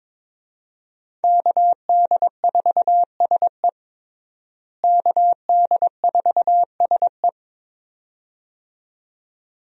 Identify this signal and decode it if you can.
Morse code, heard as a keyed tone, about 21 wpm, 700 Hz. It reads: KD4SE KD4SE
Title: USA Call Signs - 5 Characters - Repeated 2x - 22wpm